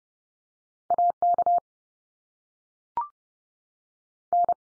Morse code, heard as a keyed tone, about 30 wpm, 700 Hz. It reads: AK N
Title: US States 30WPM